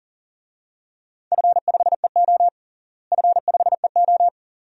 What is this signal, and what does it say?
Morse code, heard as a keyed tone, about 37 wpm, 700 Hz. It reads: F5EO F5EO
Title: DXCC-100 Call Signs - 4 Characters - Repeated 2x - 40wpm